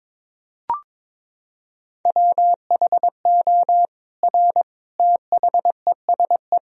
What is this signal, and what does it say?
Morse code, heard as a keyed tone, about 22 wpm, 700 Hz. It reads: WHO R THESE
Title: Sentences from Top 200 Words and Abbreviations 22wpm